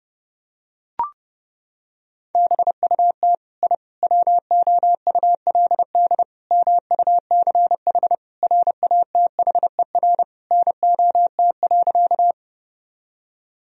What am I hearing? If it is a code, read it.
Morse code, heard as a keyed tone, about 32 wpm, 700 Hz. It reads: BUT I WOULD MUCH RATHER NOT.
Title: Sentences from Top 1000 Words - Review Part 4 - 30wpm